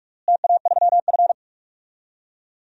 Morse code, heard as a keyed tone, about 41 wpm, 700 Hz. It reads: TA3F
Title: DXCC-100 Call Signs - 4 Characters - Repeated 2x - 45wpm